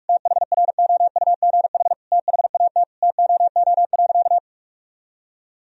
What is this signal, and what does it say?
Morse code, heard as a keyed tone, about 42 wpm, 700 Hz. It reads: THROUGH THAT TOO.
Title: Sentences from Top 1000 Words - Review Part 1 - 45wpm